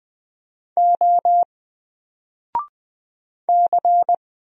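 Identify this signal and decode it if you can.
Morse code, heard as a keyed tone, about 20 wpm, 700 Hz. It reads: O C
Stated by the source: Single letter and number Morse drill at 20 wpm, from N4LCD.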